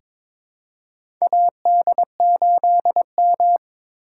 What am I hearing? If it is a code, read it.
Morse code, heard as a keyed tone, about 22 wpm, 700 Hz. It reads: AD8M
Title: USA Call Signs - 4 Characters - 6x Character Spacing - 22wpm